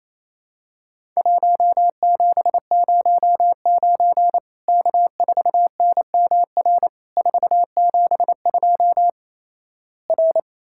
Morse code, heard as a keyed tone, about 28 wpm, 700 Hz. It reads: 1709 K4NMR 472 R
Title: Worked All Europe DX Contest - QTC Exchange - 28wpm